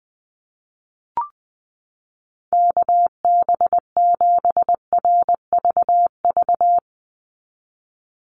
Morse code, heard as a keyed tone, about 20 wpm, 700 Hz. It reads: KB7RVV
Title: USA Call Signs 20wpm